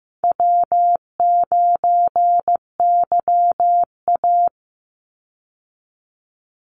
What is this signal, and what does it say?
Morse code, heard as a keyed tone, about 15 wpm, 700 Hz. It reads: W9YA
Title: Call Signs Repeated 2x - 15wpm